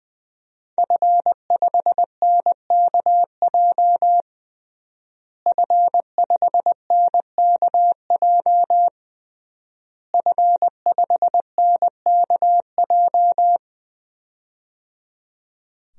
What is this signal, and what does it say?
Morse code, heard as a keyed tone, about 21 wpm, 700 Hz. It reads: F5NKJ F5NKJ F5NKJ
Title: DXCC-100 Call Signs - 5 Characters - Repeated 3x - 20wpm